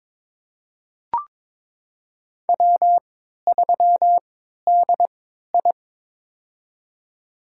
Morse code, heard as a keyed tone, about 22 wpm, spaced as for 12 wpm, 700 Hz. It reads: W3DI
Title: USA Call Signs - 4 Characters - 3x Character Spacing - 22wpm